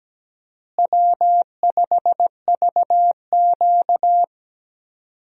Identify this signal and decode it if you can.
Morse code, heard as a keyed tone, about 17 wpm, 700 Hz. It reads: W5VQ
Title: USA Call Signs - 4 Characters - Repeated 3x - 17wpm